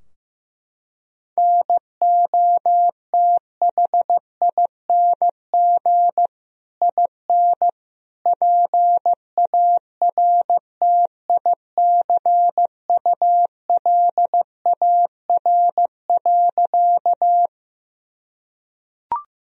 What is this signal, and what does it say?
Morse code, heard as a keyed tone, about 15 wpm, 700 Hz. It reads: NOTHING IN PARTICULAR.
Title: Sentences from Top 1000 Words - Review Part 4 - 15wpm